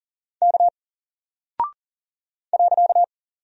Morse code, heard as a keyed tone, about 41 wpm, 700 Hz. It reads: K .
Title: Single Character (Letter:Number:Punctuation:Prosign) Alt 40wpm